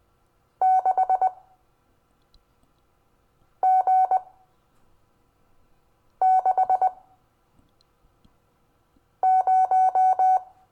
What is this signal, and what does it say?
Morse code, heard as a keyed tone, about 20 wpm, spaced as for 4 wpm, 750 Hz. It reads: 6G60
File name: cw / Lesson25 6 X